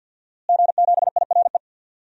Morse code, heard as a keyed tone, about 52 wpm, 700 Hz. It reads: G7IRE
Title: DXCC-100 Call Signs - 5 Characters - Repeated 2x - 50wpm